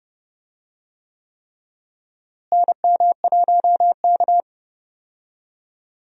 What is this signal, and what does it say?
Morse code, heard as a keyed tone, about 30 wpm, 700 Hz. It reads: NM1K
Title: USA Call Signs - 4 Characters - Repeated 3x - 30wpm